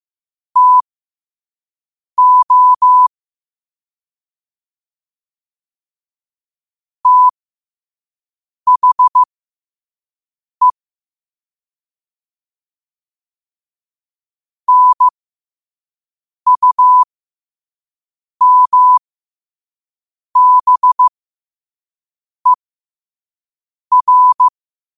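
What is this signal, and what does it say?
Morse code, heard as a keyed tone, about 15 wpm, spaced as for 5 wpm, 1000 Hz. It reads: TO THE NUMBER
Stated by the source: Morse code quote of the day at 5 wpm.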